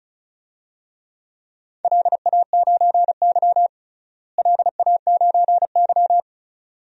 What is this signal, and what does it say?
Morse code, heard as a keyed tone, about 36 wpm, 700 Hz. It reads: LA9Y LA9Y
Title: DXCC-100 Call Signs - 4 Characters - Repeated 2x - 35wpm